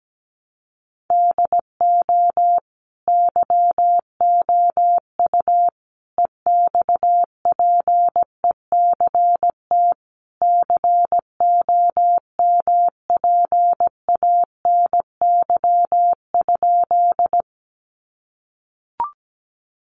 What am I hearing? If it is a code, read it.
Morse code, heard as a keyed tone, about 17 wpm, 700 Hz. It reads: DO YOU EXPECT COMPANY?
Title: Sentences from Top 1100 Words 17wpm